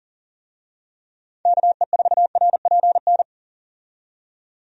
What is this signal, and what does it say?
Morse code, heard as a keyed tone, about 40 wpm, 700 Hz. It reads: KE4RPN
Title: USA Call Signs - 6 Characters - 40wpm